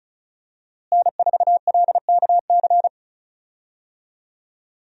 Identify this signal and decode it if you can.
Morse code, heard as a keyed tone, about 36 wpm, 700 Hz. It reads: N4LKC
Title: USA Call Signs - 5 Characters - Repeated 2x - 35wpm